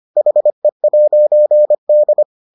Morse code, heard as a keyed tone, about 25 wpm, 600 Hz. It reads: HE'D